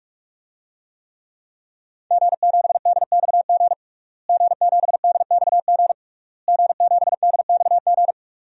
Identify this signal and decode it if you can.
Morse code, heard as a keyed tone, about 47 wpm, 700 Hz. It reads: G7DXG G7DXG G7DXG
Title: DXCC-100 Call Signs - 5 Characters - Repeated 3x - 45wpm